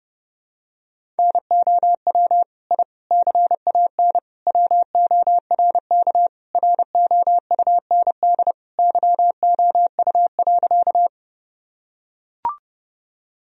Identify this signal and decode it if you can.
Morse code, heard as a keyed tone, about 32 wpm, 700 Hz. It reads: NOW I CAN WORK ROUND YOU.
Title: Sentences from Top 1000 Words - Review Part 1 - 30wpm